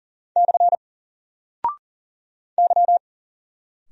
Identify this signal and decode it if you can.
Morse code, heard as a keyed tone, about 40 wpm, 700 Hz. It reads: / Y
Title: Single Character (Letter:Number:Punctuation:Prosign) 40wpm